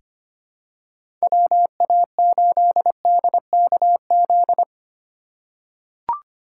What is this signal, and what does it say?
Morse code, heard as a keyed tone, about 25 wpm, 700 Hz. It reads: WA8DKZ
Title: USA Call Signs - 6 Characters - 4x Character Spacing - 25wpm